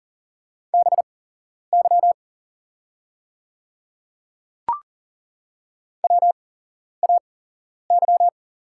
Morse code, heard as a keyed tone, about 40 wpm, spaced as for 11 wpm, 700 Hz. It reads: BY WAY
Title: 8x Character Spacing - Top 100 Words - 40wpm